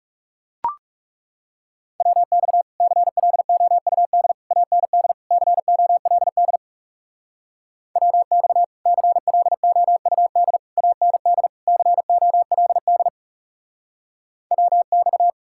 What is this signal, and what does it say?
Morse code, heard as a keyed tone, about 37 wpm, 700 Hz. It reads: WX CLOUD AND COLD WX CLOUD AND COLD WX
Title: Speed-Racing Common QSO Phrases Encore - 30wpm